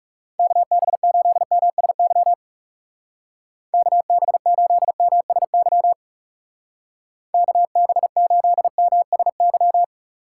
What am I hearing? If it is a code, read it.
Morse code, heard as a keyed tone, about 36 wpm, 700 Hz. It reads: KB8MSY KB8MSY KB8MSY
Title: Speed Racing - USA Call Signs 30wpm